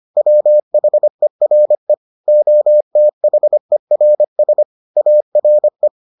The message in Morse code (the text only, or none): WHERE OTHERS ARE